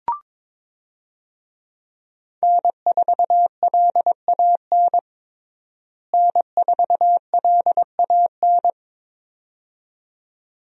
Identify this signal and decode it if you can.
Morse code, heard as a keyed tone, about 22 wpm, 700 Hz. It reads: N4LAN N4LAN
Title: USA Call Signs - 5 Characters - Repeated 2x - 22wpm